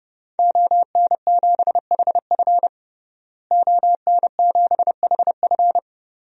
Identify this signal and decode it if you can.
Morse code, heard as a keyed tone, about 30 wpm, 700 Hz. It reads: ON7HF ON7HF
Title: DXCC-100 Call Signs - 5 Characters - Repeated 3x - 30wpm